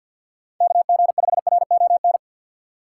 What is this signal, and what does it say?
Morse code, heard as a keyed tone, about 53 wpm, 700 Hz. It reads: KG5RON
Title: USA Call Signs - 6 Characters - 8x Character Spacing - 50wpm